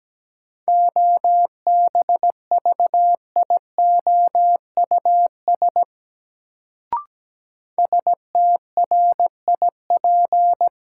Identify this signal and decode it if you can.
Morse code, heard as a keyed tone, about 17 wpm, 700 Hz. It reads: OBVIOUS STRIP
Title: Sentences from Top 1600 Words - New Word Drill - 17wpm